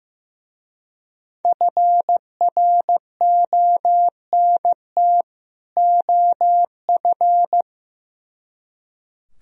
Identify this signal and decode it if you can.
Morse code, heard as a keyed tone, about 16 wpm, 700 Hz. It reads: FRONT OF
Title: Speed-Racing - Sets of 2 Words 15wpm